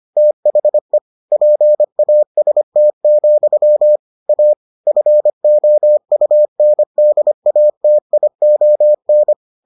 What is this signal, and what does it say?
Morse code, heard as a keyed tone, about 25 wpm, 600 Hz. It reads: THE PAST, A FOUNDATION